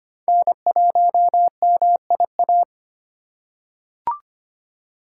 Morse code, heard as a keyed tone, about 25 wpm, 700 Hz. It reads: N1MIA
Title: USA Call Signs - 5 Characters - 8x Character Spacing - 25wpm